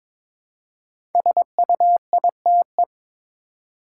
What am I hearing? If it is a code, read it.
Morse code, heard as a keyed tone, about 22 wpm, 700 Hz. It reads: SUITE
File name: 5 Letter Words - First 3 Letters Given - 22wpm